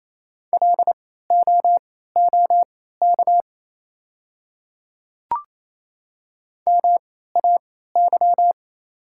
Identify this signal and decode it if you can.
Morse code, heard as a keyed tone, about 28 wpm, spaced as for 16 wpm, 690 Hz. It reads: LOOK MAY
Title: 3x Character Spacing - Top 100 Words - 28wpm